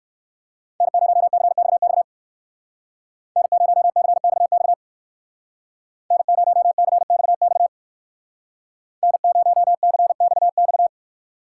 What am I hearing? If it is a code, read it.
Morse code, heard as a keyed tone, about 59 wpm, 700 Hz. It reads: N0CXX N0CXX N0CXX N0CXX